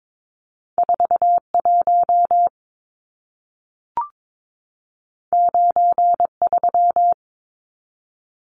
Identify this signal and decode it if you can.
Morse code, heard as a keyed tone, about 22 wpm, 700 Hz. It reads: 41 93
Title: Numbers - 2-Digit 22wpm